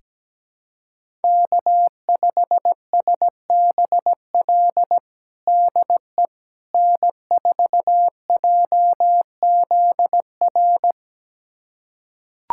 Morse code, heard as a keyed tone, about 17 wpm, 700 Hz. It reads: K5SBL DE N4JZR
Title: USA Callsign DE Callsign - 5 Characters - 17wpm